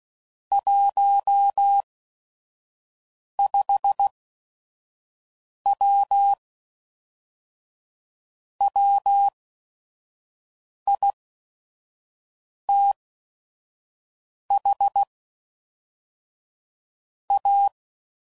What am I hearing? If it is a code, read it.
Morse code, heard as a keyed tone, about 16 wpm, spaced as for 5 wpm, 800 Hz. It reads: 15W WITH A
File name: CW Code Groups @5WPM / QSOs / WD0FBT 1